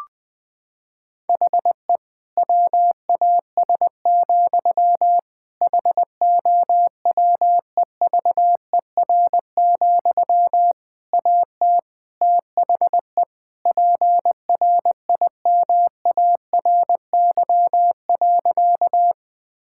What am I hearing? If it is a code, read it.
Morse code, heard as a keyed tone, about 20 wpm, 700 Hz. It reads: HE WAS, HOWEVER, AT THE PRIMARY.
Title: Sentences from Top 1000 Words - Review Part 5 - 20wpm